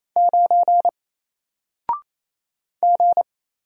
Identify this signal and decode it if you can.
Morse code, heard as a keyed tone, about 28 wpm, 700 Hz. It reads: 9 G
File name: Single-Letter-Number-28wpm